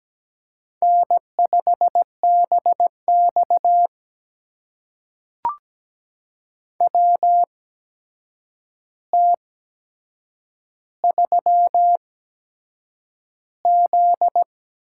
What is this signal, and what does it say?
Morse code, heard as a keyed tone, about 17 wpm, spaced as for 7 wpm, 700 Hz. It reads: N5BX WT3Z
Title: USA Call Signs - 4 Characters - 8x Character Spacing - 17wpm